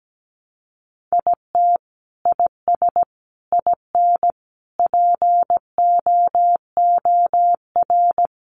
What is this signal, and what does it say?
Morse code, heard as a keyed tone, about 17 wpm, 700 Hz. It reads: IT IS IN POOR
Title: Sentences from Top 1200 Words 17wpm